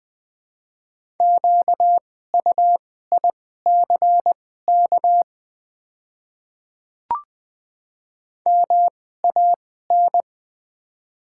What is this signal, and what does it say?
Morse code, heard as a keyed tone, about 20 wpm, spaced as for 14 wpm, 700 Hz. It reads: QUICK MAN
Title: 2x Character Spacing - Top 500 Words - 20wpm